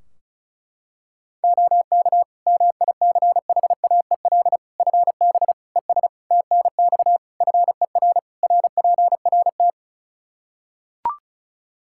Morse code, heard as a keyed tone, about 34 wpm, 700 Hz. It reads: OK MICHAEL FB ES TNX FER RPRT
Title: K1USN SST QSO Practice 35wpm